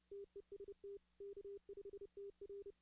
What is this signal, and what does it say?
Morse code, heard as a keyed tone, about 30 wpm, 400 Hz. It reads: TEST K5TR
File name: 2016-ARRL-DX-CW